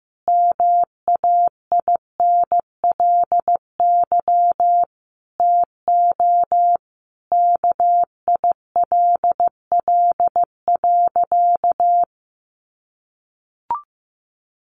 Morse code, heard as a keyed tone, about 15 wpm, 700 Hz. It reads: MAINLY TO KILL.
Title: Sentences from Top 1400 Words 15wpm